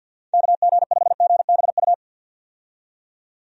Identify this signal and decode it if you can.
Morse code, heard as a keyed tone, about 51 wpm, 700 Hz. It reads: KG5GBU
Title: USA Call Signs - 6 Characters - 50wpm